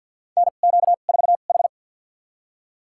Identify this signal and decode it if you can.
Morse code, heard as a keyed tone, about 49 wpm, spaced as for 35 wpm, 700 Hz. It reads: NQ4H
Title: USA Call Signs - 4 Characters - 2x Character Spacing - 50wpm